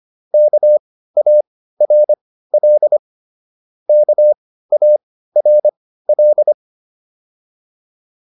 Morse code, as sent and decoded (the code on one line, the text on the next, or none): -.- .- .-. .-.. / -.- .- .-. .-..
KARL KARL